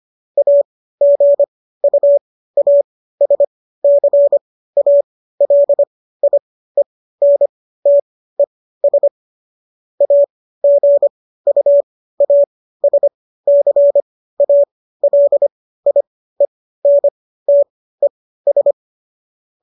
Morse code, heard as a keyed tone, about 25 wpm, spaced as for 15 wpm, 570 Hz. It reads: AGUASCALIENTES AGUASCALIENTES
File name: qso204 15